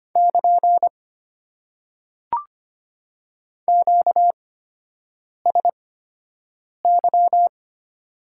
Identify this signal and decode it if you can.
Morse code, heard as a keyed tone, about 25 wpm, spaced as for 7 wpm, 690 Hz. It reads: <KN> QSY
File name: Top 90 QSO Elements - 8x Character Spacing - 25wpm